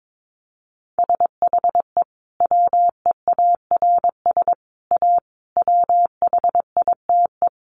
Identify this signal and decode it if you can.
Morse code, heard as a keyed tone, about 22 wpm, 700 Hz. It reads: SHE WEARS A WHITE
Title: Additional Easy Phrases 22wpm